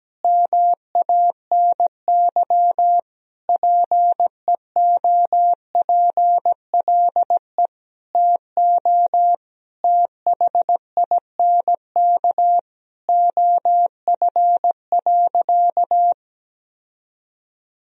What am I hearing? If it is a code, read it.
Morse code, heard as a keyed tone, about 17 wpm, 700 Hz. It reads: MANY PEOPLE TO THINK OF.
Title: Sentences from Top 1000 Words - Review Part 1 - 17wpm